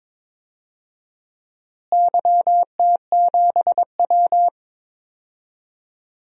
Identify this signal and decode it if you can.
Morse code, heard as a keyed tone, about 22 wpm, 700 Hz. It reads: YT7W
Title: DXCC-100 Call Signs - 4 Characters - Repeated 3x - 22wpm